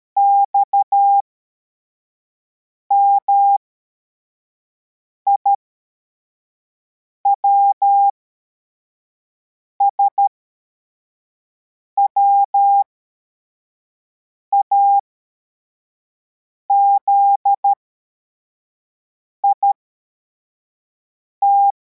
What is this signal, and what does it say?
Morse code, heard as a keyed tone, about 13 wpm, spaced as for 4 wpm, 800 Hz. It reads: XMIWS WAZIT